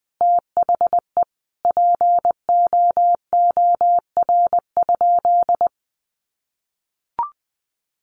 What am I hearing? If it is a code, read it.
Morse code, heard as a keyed tone, about 20 wpm, 700 Hz. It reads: THE POOR?
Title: Sentences from Top 1100 Words 20wpm